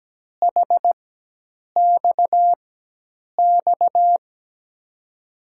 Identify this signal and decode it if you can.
Morse code, heard as a keyed tone, about 17 wpm, spaced as for 8 wpm, 700 Hz. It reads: HXX
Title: USA Call Signs - 6 Characters - 4x Character Spacing - 17wpm